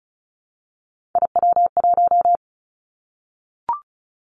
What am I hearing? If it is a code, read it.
Morse code, heard as a keyed tone, about 36 wpm, 700 Hz. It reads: IW1
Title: Callsign Prefixes for Top 50 DXCC - 35wpm